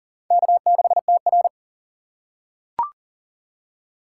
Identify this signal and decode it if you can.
Morse code, heard as a keyed tone, about 40 wpm, 700 Hz. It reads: K6TR
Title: USA Call Signs - 4x Character Spacing - 40wpm